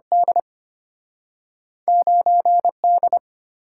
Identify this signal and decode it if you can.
Morse code, heard as a keyed tone, about 34 wpm, spaced as for 30 wpm, 700 Hz. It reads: D 9D
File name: practice / 25wpm / Speed-Racing - Two Letter-Number Combinations - 25wpm